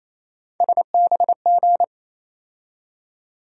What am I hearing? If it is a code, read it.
Morse code, heard as a keyed tone, about 28 wpm, 700 Hz. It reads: SBG
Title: USA Call Sign Postfixes - 3 Letters 28wpm